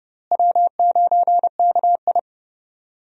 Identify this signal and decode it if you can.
Morse code, heard as a keyed tone, about 30 wpm, 700 Hz. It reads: W9KI
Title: USA Call Signs - 4 Characters - Repeated 3x - 30wpm